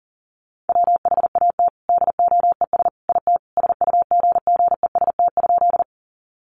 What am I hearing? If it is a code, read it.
Morse code, heard as a keyed tone, about 40 wpm, 700 Hz. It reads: WHAT DOES IT SUGGEST?
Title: Sentences from Top 1300 Words 40wpm